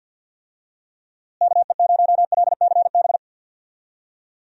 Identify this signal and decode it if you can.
Morse code, heard as a keyed tone, about 50 wpm, 700 Hz. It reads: KE0LCB
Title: USA Call Signs - 6 Characters - Repeated 3x - 50wpm